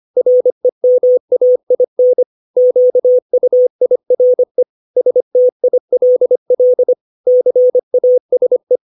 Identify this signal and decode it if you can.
Morse code, heard as a keyed tone, about 25 wpm, 500 Hz. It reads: REMAIN QUIRE STILL CASE
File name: 632 words / 632 Words 25WPM